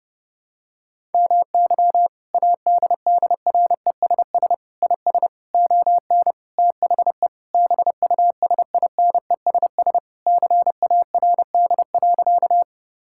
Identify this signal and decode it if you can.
Morse code, heard as a keyed tone, about 30 wpm, 700 Hz. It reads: MY ADDRESS IS ON THE BUSINESS CARD.
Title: Sentences from Top 1000 Words - Review Part 6 - 30wpm